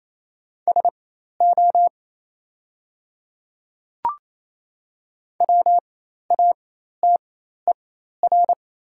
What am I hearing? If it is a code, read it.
Morse code, heard as a keyed tone, about 28 wpm, spaced as for 13 wpm, 700 Hz. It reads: SO WATER
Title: 4x Character Spacing - Top 100 Words - 28wpm